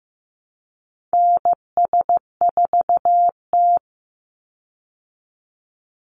Morse code, heard as a keyed tone, about 15 wpm, 700 Hz. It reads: NS4T
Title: Call Signs Repeated 3x - 15wpm